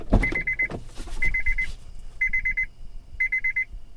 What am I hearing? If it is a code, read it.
There is no code.